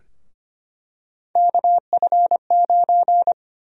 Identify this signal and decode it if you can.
Morse code, heard as a keyed tone, about 25 wpm, 700 Hz. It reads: KF9
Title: USA Call Sign Prefixes - Repeated 2x - 25wpm